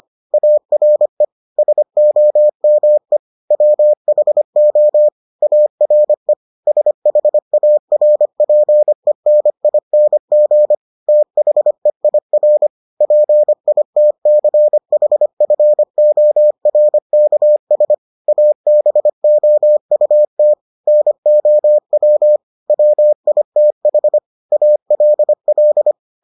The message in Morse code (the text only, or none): ARE SOME WHO ARE SHARPENING THEIR PITCHFORKS ABOUT NOW WITH ALL